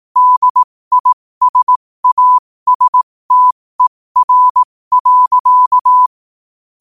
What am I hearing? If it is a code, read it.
Morse code, heard as a keyed tone, about 18 wpm, spaced as for 15 wpm, 1000 Hz. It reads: DISASTER.
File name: qotdfiles / 15wpm